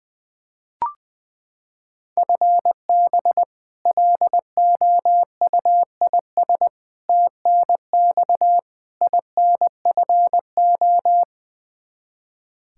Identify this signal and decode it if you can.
Morse code, heard as a keyed tone, about 20 wpm, 700 Hz. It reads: FB LOUIS TNX INFO
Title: K1USN SST QSO Practice 20wpm